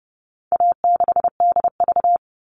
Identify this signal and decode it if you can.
Morse code, heard as a keyed tone, about 30 wpm, 700 Hz. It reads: A6DV
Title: All Cut Numbers - 4-Digit 30wpm